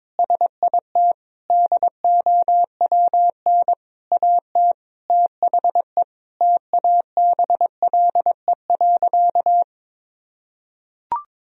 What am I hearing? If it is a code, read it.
Morse code, heard as a keyed tone, about 21 wpm, 700 Hz. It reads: SIT DOWN AT THE TABLE.
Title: Sentences from Top 1000 Words - Review Part 2 - 22wpm